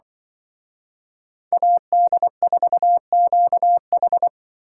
Morse code, heard as a keyed tone, about 24 wpm, 700 Hz. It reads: AD4QH
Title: Speed Racing - USA Call Signs 20wpm